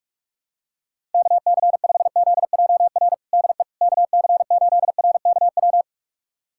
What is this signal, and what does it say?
Morse code, heard as a keyed tone, about 46 wpm, 700 Hz. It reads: KC5ZJR DE KC8RKW
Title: USA Callsign DE Callsign - 6 Characters - 45wpm